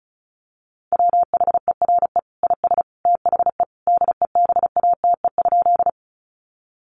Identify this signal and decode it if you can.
Morse code, heard as a keyed tone, about 34 wpm, 700 Hz. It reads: WHERE IS THE DEBATE?
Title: Sentences from Top 1100 Words 35wpm